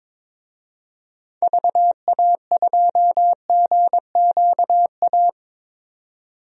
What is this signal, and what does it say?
Morse code, heard as a keyed tone, about 22 wpm, 700 Hz. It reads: VA2GQA
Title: DXCC-100 Call Signs - 6 Characters - 22wpm